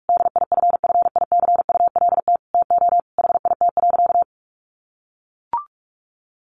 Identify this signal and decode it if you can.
Morse code, heard as a keyed tone, about 44 wpm, 700 Hz. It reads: DIFFICULT TO HIT.
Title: Sentences from Top 1200 Words 45wpm